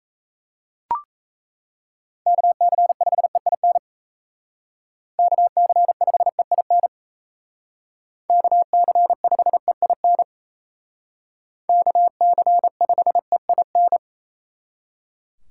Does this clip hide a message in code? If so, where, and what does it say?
Yes: Morse code from 2.3 to 14.0 s, heard as a keyed tone, about 36 wpm, 700 Hz. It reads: KC5EIN KC5EIN KC5EIN KC5EIN